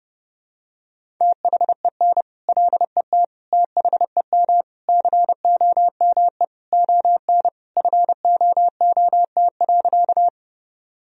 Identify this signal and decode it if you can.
Morse code, heard as a keyed tone, about 30 wpm, 700 Hz. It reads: THEN LET THEM COME ON FOOT.
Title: Sentences from Top 1000 Words - Review Part 3 - 30wpm